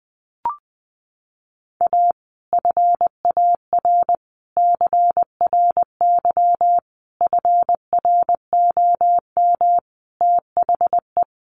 Morse code, heard as a keyed tone, about 20 wpm, 700 Hz. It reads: A FAR CRY FROM THE
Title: Sets of 5 Words Encore 20wpm